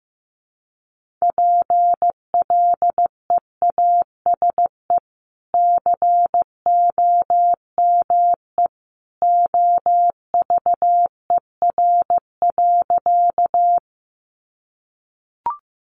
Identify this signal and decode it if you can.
Morse code, heard as a keyed tone, about 15 wpm, 700 Hz. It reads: PLEASE COME OVER.
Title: Sentences from Top 1500 Words 15wpm